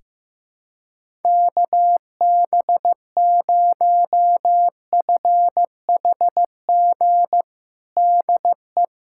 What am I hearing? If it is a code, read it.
Morse code, heard as a keyed tone, about 16 wpm, 700 Hz. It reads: KB0FHG DE
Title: USA Callsign DE Callsign - 6 Characters - 15wpm